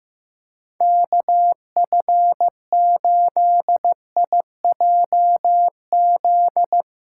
Morse code, heard as a keyed tone, about 16 wpm, 700 Hz. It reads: KF8IJZ
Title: Speed Racing - USA 2x3 Call Signs 15wpm